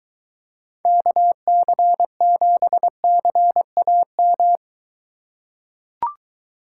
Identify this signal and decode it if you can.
Morse code, heard as a keyed tone, about 23 wpm, 700 Hz. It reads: KC7CAM